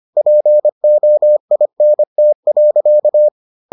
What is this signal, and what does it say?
Morse code, heard as a keyed tone, about 25 wpm, 600 Hz. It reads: POINT.